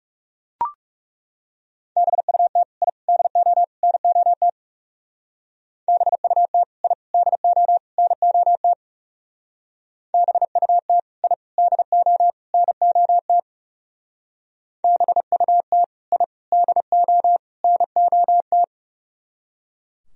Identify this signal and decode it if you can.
Morse code, heard as a keyed tone, about 36 wpm, 700 Hz. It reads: BUT I DO NOT BUT I DO NOT BUT I DO NOT BUT I DO NOT
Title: Speed-Racing - Sets of 4 Words 30wpm